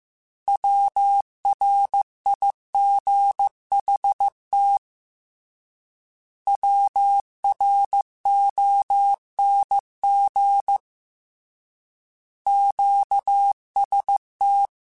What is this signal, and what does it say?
Morse code, heard as a keyed tone, about 15 wpm, 800 Hz. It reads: WRIGHT WRONG QST
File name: Short Runs / Short Runs-mp3s / Words @13